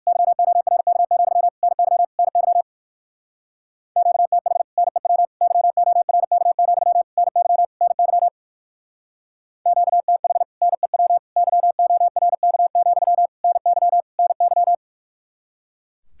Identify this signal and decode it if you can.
Morse code, heard as a keyed tone, about 61 wpm, 700 Hz. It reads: YORK, NY NY QTH NEW YORK, NY NY QTH NEW YORK, NY NY